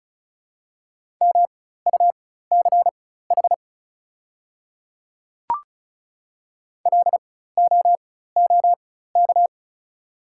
Morse code, heard as a keyed tone, about 36 wpm, spaced as for 16 wpm, 690 Hz. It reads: MUCH LOOK
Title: 4x Character Spacing - Top 300 Words - 35wpm